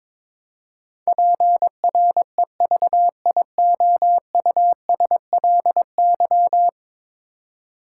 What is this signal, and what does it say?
Morse code, heard as a keyed tone, about 22 wpm, 700 Hz. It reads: PREVIOUSLY
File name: practice / 22wpm / 10 Letter Words - First 3 Letters Given - 22wpm